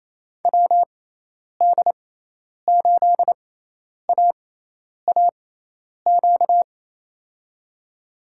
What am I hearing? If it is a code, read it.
Morse code, heard as a keyed tone, about 27 wpm, spaced as for 10 wpm, 700 Hz. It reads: WD8AAQ
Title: USA Call Signs - 6 Characters - 6x Character Spacing - 28wpm